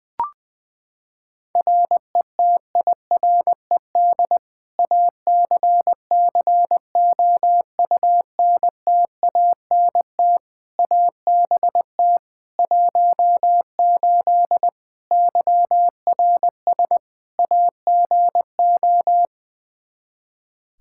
Morse code, heard as a keyed tone, about 20 wpm, 700 Hz. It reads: RETIRED ACCOUNTANT ABT 18 YRS AGO
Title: K1USN SST QSO Practice 20wpm